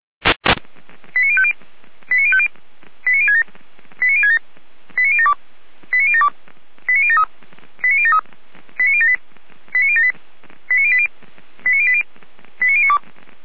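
Five-tone selective call (ZVEI), heard as quick runs of five tones, tones of about 70 ms. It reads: 89050 89050 89067 89067 89072 89072 89073 89073 89078 89078 89080 89080 89092
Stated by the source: ZVEI five-tone selective call received on an SDR.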